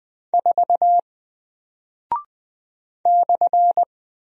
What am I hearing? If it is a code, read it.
Morse code, heard as a keyed tone, about 20 wpm, 700 Hz. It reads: 4 /
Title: Single Character (Letter:Number:Punctuation:Prosign) Alt 20wpm